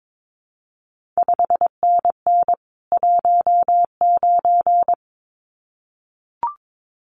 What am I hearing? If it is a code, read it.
Morse code, heard as a keyed tone, about 22 wpm, 700 Hz. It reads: 5NN 19
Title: CQ WW DX Contest Exchange Encore - 22wpm